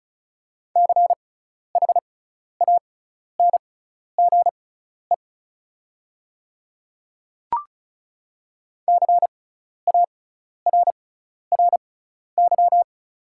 Morse code, heard as a keyed tone, about 36 wpm, spaced as for 12 wpm, 690 Hz. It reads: CHANGE CARRY
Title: 6x Character Spacing - Top 300 Words - 35wpm